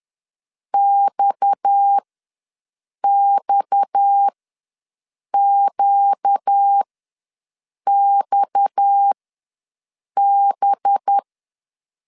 Morse code, heard as a keyed tone, about 10 wpm, spaced as for 6 wpm, 800 Hz. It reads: XXQXB
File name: LEZIONI / 6 PKX0 / 6 PKX ws 10wpm 3706